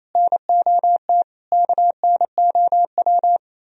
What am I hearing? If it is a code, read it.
Morse code, heard as a keyed tone, about 37 wpm, 700 Hz. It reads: NOT KNOW